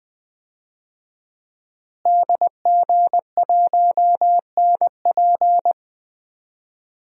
Morse code, heard as a keyed tone, about 20 wpm, 700 Hz. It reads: DG1NP
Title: DXCC-100 Call Signs - 5 Characters - Repeated 2x - 20wpm